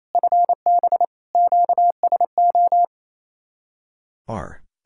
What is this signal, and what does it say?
Morse code, heard as a keyed tone, about 27 wpm, 700 Hz. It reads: FB QSO
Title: Speed-Racing Common QSO Phrases - 28wpm